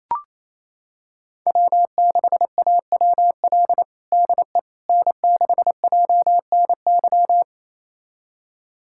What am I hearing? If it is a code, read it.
Morse code, heard as a keyed tone, about 28 wpm, 700 Hz. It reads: W6AWL DE N6JNY
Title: USA Callsign DE Callsign - 5 Characters - 28wpm